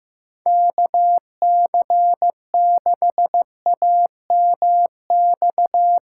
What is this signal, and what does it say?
Morse code, heard as a keyed tone, about 15 wpm, 700 Hz. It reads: KC6AMX
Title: USA Call Signs - 4x Character Spacing - 15wpm